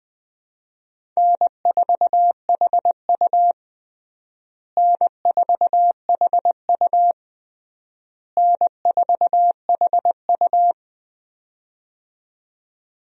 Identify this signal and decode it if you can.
Morse code, heard as a keyed tone, about 19 wpm, 700 Hz. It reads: N4HU N4HU N4HU
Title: USA Call Signs - 4 Characters - Repeated 3x - 20wpm